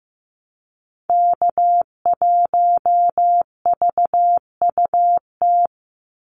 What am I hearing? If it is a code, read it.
Morse code, heard as a keyed tone, about 15 wpm, 700 Hz. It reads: K1VUT
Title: Call Signs Repeated 2x - 15wpm